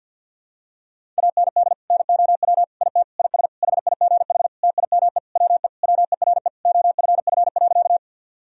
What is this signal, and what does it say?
Morse code, heard as a keyed tone, about 50 wpm, 700 Hz. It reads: AND NOW IT IS HIGH TIME WE WERE OFF.
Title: Sentences from Top 1000 Words - Review Part 1 - 50wpm